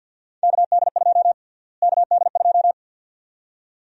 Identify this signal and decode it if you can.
Morse code, heard as a keyed tone, about 53 wpm, 700 Hz. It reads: KD2 KD2
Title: USA Call Sign Prefixes - Repeated 2x - 50wpm